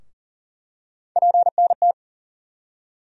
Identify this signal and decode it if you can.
Morse code, heard as a keyed tone, about 40 wpm, 700 Hz. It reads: PNT
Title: USA Call Sign Postfixes - 3 Letters 40wpm